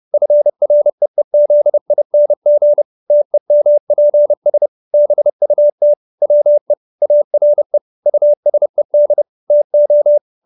Morse code, heard as a keyed tone, about 30 wpm, 590 Hz. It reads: FREEZING TEMPS BUT WE ARE USED TO